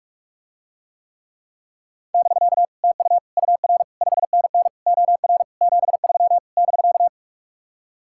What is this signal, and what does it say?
Morse code, heard as a keyed tone, about 49 wpm, 700 Hz. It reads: <BK> TU UR 5NN OR 73 <BK>